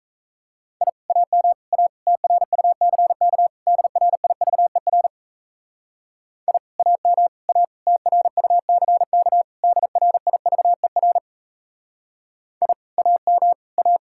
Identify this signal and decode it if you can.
Morse code, heard as a keyed tone, about 38 wpm, 700 Hz. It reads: I AM A TRUCK DRIVER I AM A TRUCK DRIVER I AM A